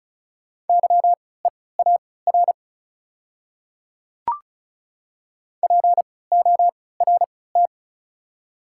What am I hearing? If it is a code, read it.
Morse code, heard as a keyed tone, about 36 wpm, spaced as for 20 wpm, 700 Hz. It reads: YEAR PORT
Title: 3x Character Spacing - Top 200 Words - 35wpm